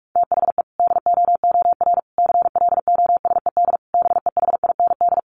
Morse code, heard as a keyed tone, about 44 wpm, 700 Hz. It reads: THE DOOR CLOSED BEHIND
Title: Sentences from Top 1400 Words 45wpm